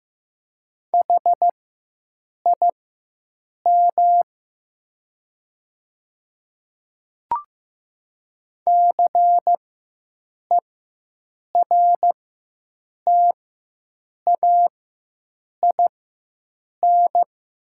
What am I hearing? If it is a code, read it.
Morse code, heard as a keyed tone, about 15 wpm, spaced as for 7 wpm, 690 Hz. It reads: HIM CERTAIN
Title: 4x Character Spacing - Top 500 Words - 15wpm